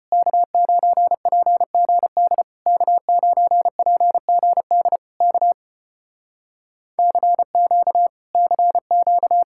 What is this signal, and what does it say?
Morse code, heard as a keyed tone, about 30 wpm, 700 Hz. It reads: K9PGD K9PGD K CQ CQ
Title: Speed-Racing Common QSO Phrases - 25wpm